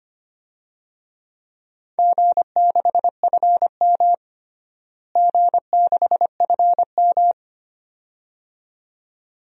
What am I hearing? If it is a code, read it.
Morse code, heard as a keyed tone, about 25 wpm, 700 Hz. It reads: G6FM G6FM
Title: DXCC-100 Call Signs - 4 Characters - Repeated 2x - 25wpm